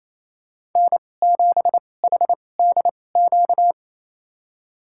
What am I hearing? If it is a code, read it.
Morse code, heard as a keyed tone, about 28 wpm, spaced as for 20 wpm, 700 Hz. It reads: N7HDQ
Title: USA Call Signs - 2x Character Spacing - 28wpm